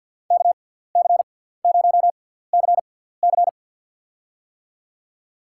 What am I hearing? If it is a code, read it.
Morse code, heard as a keyed tone, about 52 wpm, spaced as for 17 wpm, 700 Hz. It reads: KC0CC
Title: USA Call Signs - 5 Characters - 6x Character Spacing - 50wpm